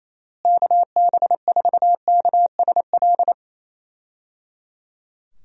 Morse code, heard as a keyed tone, about 28 wpm, 700 Hz. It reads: KB4KSL
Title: USA Call Signs - 6 Characters - Repeated 3x - 28wpm